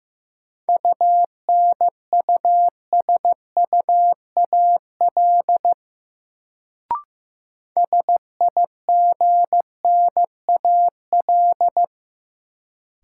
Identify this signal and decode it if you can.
Morse code, heard as a keyed tone, about 15 wpm, 700 Hz. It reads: UNUSUAL SIGNAL
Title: Sentences from Top 1400 Words - New Word Drill - 15wpm